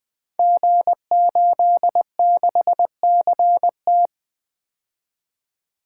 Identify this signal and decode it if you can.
Morse code, heard as a keyed tone, about 20 wpm, 700 Hz. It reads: G86CT
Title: Code Groups - 5 Letter:Numbers - 20wpm